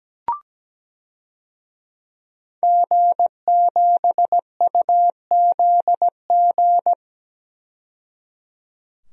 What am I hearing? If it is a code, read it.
Morse code, heard as a keyed tone, about 17 wpm, 700 Hz. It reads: G7UZG
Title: DXCC-100 Call Signs - 5 Characters - 17wpm